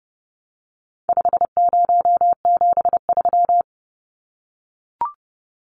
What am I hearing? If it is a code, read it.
Morse code, heard as a keyed tone, about 30 wpm, 700 Hz. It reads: 5073
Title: Numbers - 4-Digit 30wpm